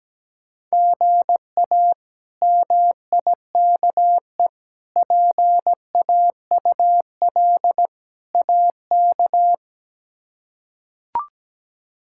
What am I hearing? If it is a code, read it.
Morse code, heard as a keyed tone, about 17 wpm, 700 Hz. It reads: GA MIKE PAUL AK
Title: K1USN SST QSO Practice - 17wpm